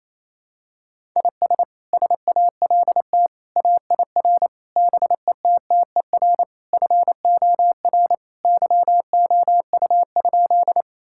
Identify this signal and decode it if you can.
Morse code, heard as a keyed tone, about 28 wpm, 700 Hz. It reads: IS SALT AIR BETTER FOR YOU?
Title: Sentences from Top 1000 Words - Review Part 5 - 28wpm